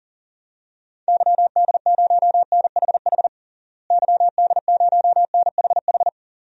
Morse code, heard as a keyed tone, about 40 wpm, 700 Hz. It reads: YD0NHH YD0NHH
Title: DXCC-100 Call Signs - 6 Characters - Repeated 2x - 40wpm